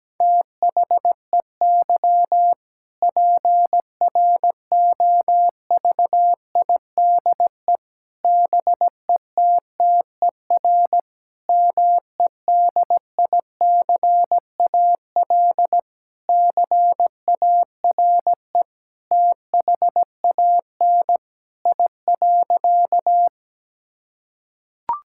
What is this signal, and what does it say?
Morse code, heard as a keyed tone, about 17 wpm, 700 Hz. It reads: THEY PROVIDE BETTER MEDICAL CARE THAN I.
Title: Sentences from Top 1000 Words - Review Part 5 - 17wpm